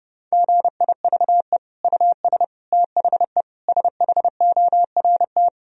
Morse code, heard as a keyed tone, about 30 wpm, 700 Hz. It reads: GIVE US THE SHORT
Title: Sentences from Top 1000 Words - Review Part 4 - 30wpm